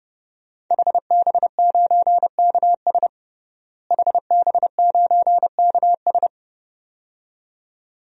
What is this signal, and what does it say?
Morse code, heard as a keyed tone, about 30 wpm, 700 Hz. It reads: HB9KS HB9KS
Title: DXCC-100 Call Signs - 5 Characters - Repeated 2x - 30wpm